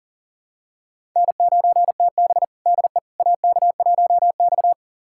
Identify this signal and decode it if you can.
Morse code, heard as a keyed tone, about 40 wpm, 700 Hz. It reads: N9TB DE AK1X
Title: USA Callsign DE Callsign - 4 Characters - 40wpm